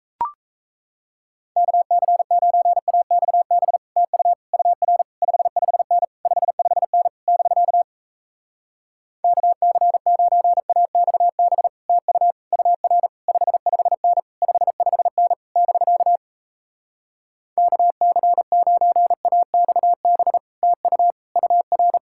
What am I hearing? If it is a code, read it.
Morse code, heard as a keyed tone, about 38 wpm, 700 Hz. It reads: KC9AXB TU UR 55N 55N <BK> KC9AXB TU UR 55N 55N <BK> KC9AXB TU UR